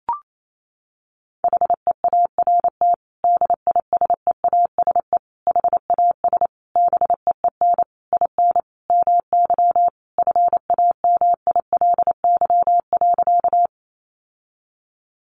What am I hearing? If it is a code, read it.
Morse code, heard as a keyed tone, about 28 wpm, 700 Hz. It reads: HEART DISEASE HAS BEEN IN MY FAMILY.
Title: Sentences from Top 1200 Words 28wpm